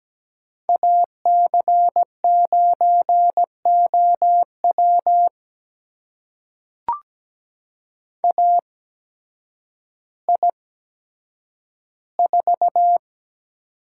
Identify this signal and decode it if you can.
Morse code, heard as a keyed tone, about 17 wpm, spaced as for 9 wpm, 700 Hz. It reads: AC9OW AI4
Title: USA Call Signs - 5 Characters - 8x Character Spacing - 17wpm